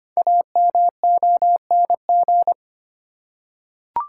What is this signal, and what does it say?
Morse code, heard as a keyed tone, about 25 wpm, 700 Hz. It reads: AMONG
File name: 5 Letter Words - First 3 Letters Given - 25wpm